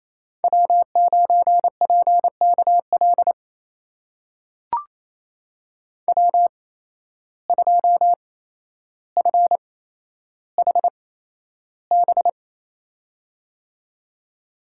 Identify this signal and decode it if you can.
Morse code, heard as a keyed tone, about 28 wpm, spaced as for 12 wpm, 700 Hz. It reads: W9PKL W2FHB
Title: USA Call Signs - 5 Characters - 8x Character Spacing - 28wpm